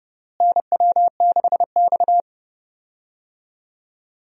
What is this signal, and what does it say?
Morse code, heard as a keyed tone, about 30 wpm, 700 Hz. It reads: NW6X
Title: USA Call Signs - 4 Characters - Repeated 2x - 30wpm